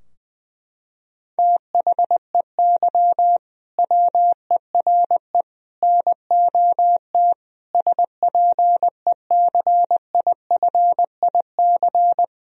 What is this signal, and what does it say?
Morse code, heard as a keyed tone, about 20 wpm, 700 Hz. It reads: THEY WERE NOT SPECIFIC
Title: Sentences from Top 1000 Words - Review Part 3 - 20wpm